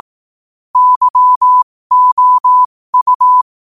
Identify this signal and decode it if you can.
Morse code, heard as a keyed tone, about 18 wpm, spaced as for 15 wpm, 1000 Hz. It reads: YOU
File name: qotdfiles / 15wpm